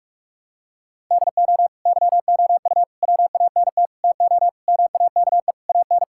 Morse code, heard as a keyed tone, about 46 wpm, 700 Hz. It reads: DO YOU WANT TO MAKE AN
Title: Sentences from Top 1000 Words - Review Part 6 - 45wpm